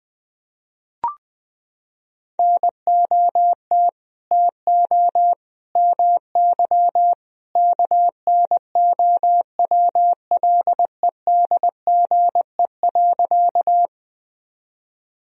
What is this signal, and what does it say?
Morse code, heard as a keyed tone, about 20 wpm, 700 Hz. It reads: NOT TO MY KNOWLEDGE.
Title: Sentences from Top 1000 Words - Review Part 3 - 20wpm